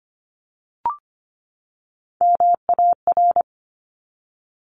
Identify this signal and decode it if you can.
Morse code, heard as a keyed tone, about 25 wpm, 700 Hz. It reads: MAR
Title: ARRL Sections 25wpm